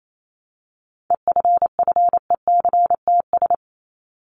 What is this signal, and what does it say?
Morse code, heard as a keyed tone, about 28 wpm, 700 Hz. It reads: EFFECTS
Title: ES:S Suffix - Top 100 Words with Base Words 28wpm